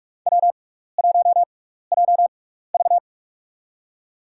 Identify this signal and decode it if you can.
Morse code, heard as a keyed tone, about 47 wpm, spaced as for 16 wpm, 700 Hz. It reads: W1JV
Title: USA Call Signs - 4 Characters - 6x Character Spacing - 45wpm